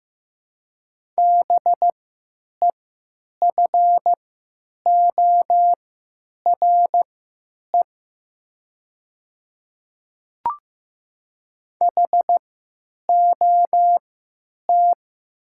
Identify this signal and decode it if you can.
Morse code, heard as a keyed tone, about 15 wpm, spaced as for 9 wpm, 700 Hz. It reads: BEFORE HOT
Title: 3x Character Spacing - Top 200 Words - 15wpm